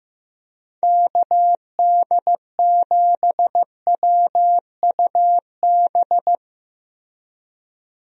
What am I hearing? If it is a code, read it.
Morse code, heard as a keyed tone, about 15 wpm, 700 Hz. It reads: KD7WUB
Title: USA Call Signs - 6 Characters - Repeated 3x - 15wpm